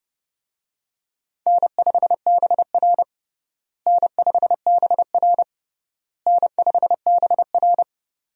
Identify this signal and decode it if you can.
Morse code, heard as a keyed tone, about 29 wpm, 700 Hz. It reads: N5BR N5BR N5BR
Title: USA Call Signs - 4 Characters - Repeated 3x - 30wpm